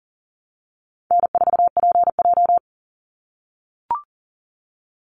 Morse code, heard as a keyed tone, about 40 wpm, 700 Hz. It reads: N4PJ
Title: Call Signs - 40wpm